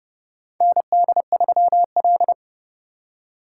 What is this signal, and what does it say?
Morse code, heard as a keyed tone, about 32 wpm, 700 Hz. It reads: ND3L
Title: K1USN SST QSO Practice - 30wpm